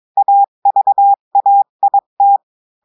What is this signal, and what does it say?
Morse code, heard as a keyed tone, about 22 wpm, 800 Hz. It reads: AVAIT
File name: cours-du-mercredi / audios / cours-853-du-2-mai-2024-etxte chiffre et texte clair-a-20-wpm